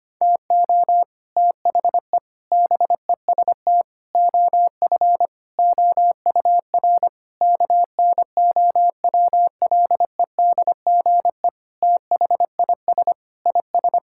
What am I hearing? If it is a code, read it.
Morse code, heard as a keyed tone, about 25 wpm, 700 Hz. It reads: TO THE BEST OF OUR KNOWLEDGE THIS IS
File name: practice / 25wpm / Sentences with 9-Word Phrases - 25wpm